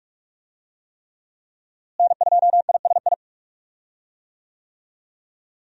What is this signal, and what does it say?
Morse code, heard as a keyed tone, about 43 wpm, 700 Hz. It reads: N2ISI
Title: USA Call Signs - 5 Characters - Repeated 3x - 45wpm